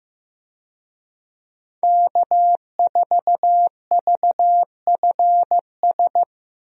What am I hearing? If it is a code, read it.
Morse code, heard as a keyed tone, about 16 wpm, 700 Hz. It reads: K4VFS
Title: USA Call Signs - 5 Characters - 15wpm